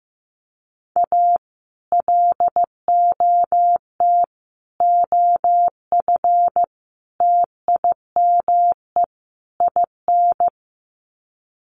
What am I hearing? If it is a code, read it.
Morse code, heard as a keyed tone, about 15 wpm, 700 Hz. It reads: A LOT OF TIME IN
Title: Sets of 5 Words Encore2 15wpm